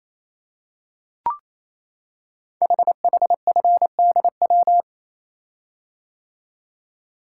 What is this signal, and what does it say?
Morse code, heard as a keyed tone, about 27 wpm, 700 Hz. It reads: HHFDW
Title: Code Groups - 5 Letters - 28wpm